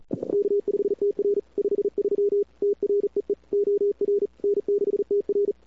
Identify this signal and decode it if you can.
Morse code, heard as a keyed tone, about 39 wpm, 410 Hz. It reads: K5TR 53 TREE OR N6TR
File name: cw-sprint-k5tr-op-n5rz